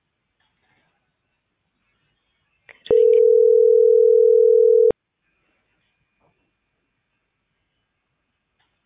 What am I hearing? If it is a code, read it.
Ringback tone (North America), 440 + 480 Hz.